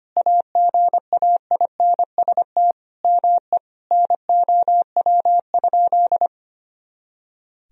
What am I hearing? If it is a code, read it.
Morse code, heard as a keyed tone, about 25 wpm, 700 Hz. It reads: AGAINST ME NOW?
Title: Sentences from Top 400 Words and Abbreviations 25wpm